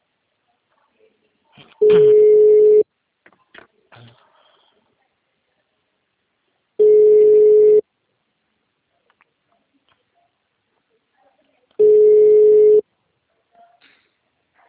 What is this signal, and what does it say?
Ringback tone (Europe, 425 Hz).
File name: out- -2007- - - .10